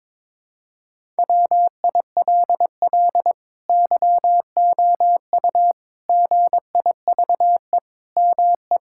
Morse code, heard as a keyed tone, about 21 wpm, 700 Hz. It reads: WILL YOU GIVE ME
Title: Sentences from Top 1600 Words 22wpm